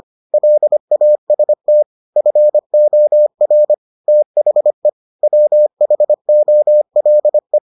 Morse code, heard as a keyed tone, about 25 wpm, 600 Hz. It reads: LAST FOR THE WHOLE